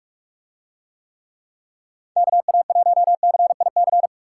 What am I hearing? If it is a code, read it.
Morse code, heard as a keyed tone, about 41 wpm, 700 Hz. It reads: KA1CIC